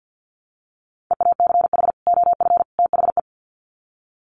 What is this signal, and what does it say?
Morse code, heard as a keyed tone, about 50 wpm, 700 Hz. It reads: EACH OF THE
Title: Sets of 3 Words 50wpm